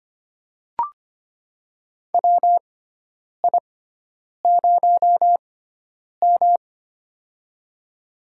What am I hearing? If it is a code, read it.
Morse code, heard as a keyed tone, about 25 wpm, spaced as for 9 wpm, 700 Hz. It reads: WI0M
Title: USA Call Signs - 6x Character Spacing - 25wpm